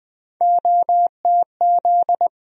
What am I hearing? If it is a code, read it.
Morse code, heard as a keyed tone, about 20 wpm, 700 Hz. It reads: OTZ
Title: USA Call Sign Postfixes - 3 Letters 20wpm